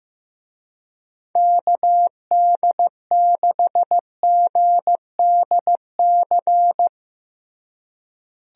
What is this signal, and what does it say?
Morse code, heard as a keyed tone, about 15 wpm, 700 Hz. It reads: KD6GDC
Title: USA Call Signs - 6 Characters - Repeated 2x - 15wpm